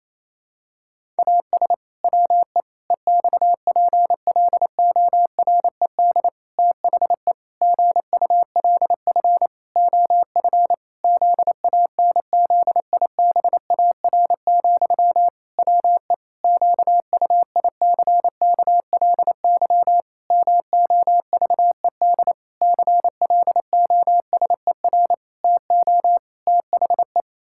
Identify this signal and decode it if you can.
Morse code, heard as a keyed tone, about 27 wpm, 700 Hz. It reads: AS WE EXPLORED THE GULF OF ZANZIBAR, WE QUICKLY MOVED CLOSER TO THE
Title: Pangram Challenge v2 - 28wpm